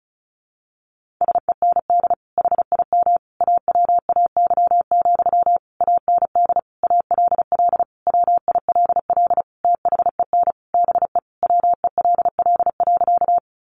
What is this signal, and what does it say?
Morse code, heard as a keyed tone, about 34 wpm, 700 Hz. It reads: SEND HIM AWAY, AND ALL WILL THEN BE WELL.
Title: Sentences from Top 1500 Words 35wpm